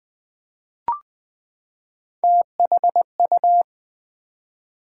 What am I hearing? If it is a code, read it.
Morse code, heard as a keyed tone, about 20 wpm, 700 Hz. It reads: THU
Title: Build A Word - 3 Letters At A Time - 8-Letter Words - 20wpm